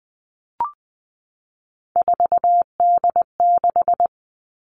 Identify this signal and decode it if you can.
Morse code, heard as a keyed tone, about 20 wpm, 700 Hz. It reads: 4D6
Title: All Cut Numbers - 3-Digit 20wpm